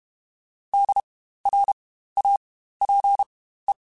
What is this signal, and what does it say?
Morse code, heard as a keyed tone, about 32 wpm, spaced as for 15 wpm, 800 Hz. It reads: DRAPE